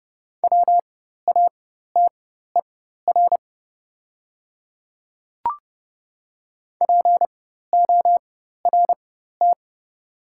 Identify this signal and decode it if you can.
Morse code, heard as a keyed tone, about 30 wpm, spaced as for 14 wpm, 700 Hz. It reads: WATER PORT
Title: 4x Character Spacing - Top 200 Words - 30wpm